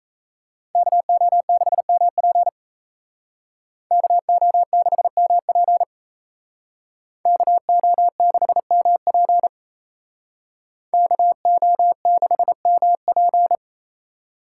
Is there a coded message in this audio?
Yes: Morse code from 0.8 to 13.6 s, heard as a keyed tone, about 36 wpm, 700 Hz. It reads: KO6MP KO6MP KO6MP KO6MP